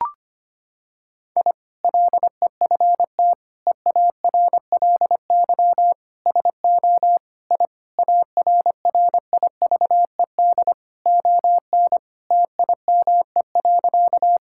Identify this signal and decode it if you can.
Morse code, heard as a keyed tone, about 25 wpm, 700 Hz. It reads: I LEFT EARLY SO I ARRIVED ON TIME.
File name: practice / 25wpm / Sentences with 2-Word Phrases - 25wpm